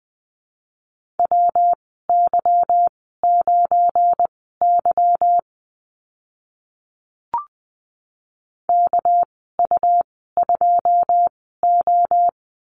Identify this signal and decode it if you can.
Morse code, heard as a keyed tone, about 20 wpm, spaced as for 15 wpm, 700 Hz. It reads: WY9Y KU2O
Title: 200 Lesson 32g - Letter k - Call Signs 20wpm